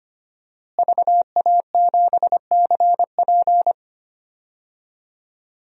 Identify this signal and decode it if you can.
Morse code, heard as a keyed tone, about 25 wpm, 700 Hz. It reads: VA7CP
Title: DXCC-100 Call Signs - 5 Characters - Repeated 3x - 25wpm